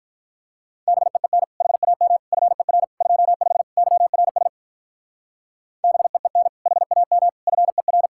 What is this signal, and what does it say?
Morse code, heard as a keyed tone, about 44 wpm, 700 Hz. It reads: BEEN HAM FER 25 YRS BEEN HAM FER